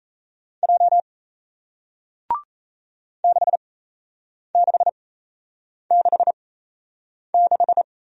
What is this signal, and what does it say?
Morse code, heard as a keyed tone, about 37 wpm, spaced as for 8 wpm, 700 Hz. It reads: J 6666